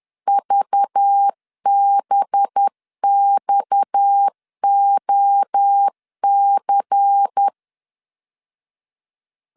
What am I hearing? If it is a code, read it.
Morse code, heard as a keyed tone, about 10 wpm, 800 Hz. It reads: VBXOC